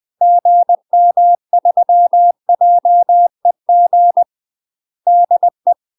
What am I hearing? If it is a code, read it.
Morse code, heard as a keyed tone, about 20 wpm, 700 Hz. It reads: GM3JEG DE
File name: UK MASTER QSOs TRAINING / 20WPM / UK MASTER QSOs 0023